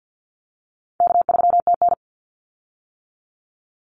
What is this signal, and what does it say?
Morse code, heard as a keyed tone, about 50 wpm, 700 Hz. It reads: K3TN
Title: Call Signs - 50wpm